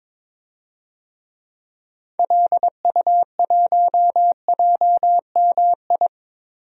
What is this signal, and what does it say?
Morse code, heard as a keyed tone, about 21 wpm, 700 Hz. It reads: LU1JMI
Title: DXCC-100 Call Signs - 6 Characters - Repeated 2x - 22wpm